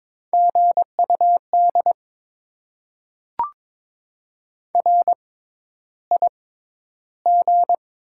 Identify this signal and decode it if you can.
Morse code, heard as a keyed tone, about 22 wpm, spaced as for 11 wpm, 700 Hz. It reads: GUD RIG
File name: Top 20 QSO Elements - 6x Character Spacing - 22wpm